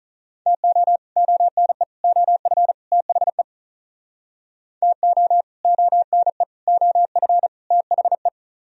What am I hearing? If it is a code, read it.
Morse code, heard as a keyed tone, about 45 wpm, 700 Hz. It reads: TO ONE OF THE TO ONE OF THE